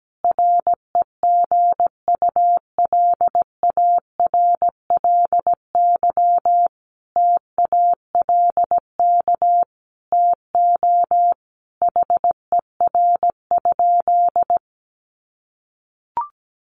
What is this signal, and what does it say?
Morse code, heard as a keyed tone, about 17 wpm, 700 Hz. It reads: REGULARLY TALK TO HER?
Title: Sentences from Top 1400 Words 17wpm